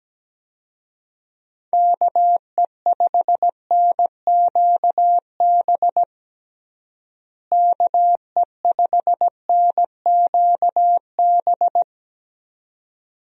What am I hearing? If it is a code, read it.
Morse code, heard as a keyed tone, about 17 wpm, 700 Hz. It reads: KE5NQB KE5NQB
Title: USA Call Signs - 6 Characters - Repeated 2x - 17wpm